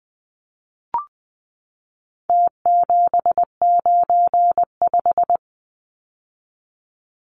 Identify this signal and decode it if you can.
Morse code, heard as a keyed tone, about 20 wpm, 700 Hz. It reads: T795
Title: Numbers - Contest Style - 4-Digit 20wpm